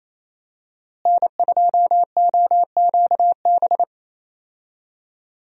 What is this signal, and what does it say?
Morse code, heard as a keyed tone, about 28 wpm, 700 Hz. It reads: N2OQB
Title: USA Call Signs - 5 Characters - Repeated 3x - 28wpm